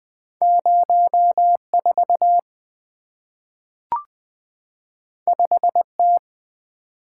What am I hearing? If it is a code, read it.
Morse code, heard as a keyed tone, about 20 wpm, 700 Hz. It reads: 04 5T
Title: Two Letter-Number Combinations - 20wpm